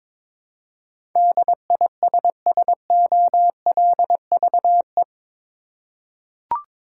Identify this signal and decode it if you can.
Morse code, heard as a keyed tone, about 22 wpm, 700 Hz. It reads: DISSOLVE
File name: practice / 22wpm / Eight Letter Words - 22wpm